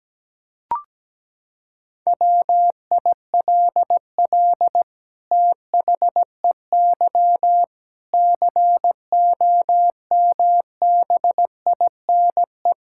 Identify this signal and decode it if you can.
Morse code, heard as a keyed tone, about 17 wpm, 700 Hz. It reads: WILL THEY COMBINE
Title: Sentences from Top 1600 Words 17wpm